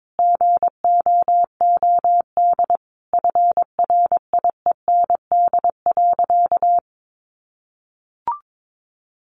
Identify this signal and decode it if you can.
Morse code, heard as a keyed tone, about 22 wpm, 700 Hz. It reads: GOOD FRIEND.
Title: Sentences from Top 600 Words 22wpm